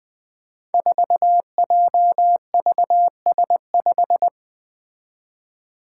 Morse code, heard as a keyed tone, about 20 wpm, 700 Hz. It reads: 4JVS5
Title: Five Challenging Letter-Number Combinations 20wpm